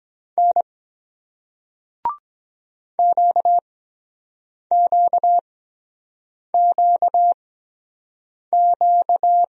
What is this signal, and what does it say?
Morse code, heard as a keyed tone, about 23 wpm, spaced as for 7 wpm, 700 Hz. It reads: N QQQQ